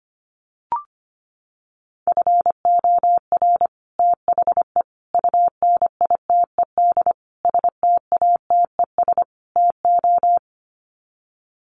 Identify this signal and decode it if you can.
Morse code, heard as a keyed tone, about 25 wpm, 700 Hz. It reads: FOR THE UNITED STATES TO
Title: Sets of 5 Words Encore 25wpm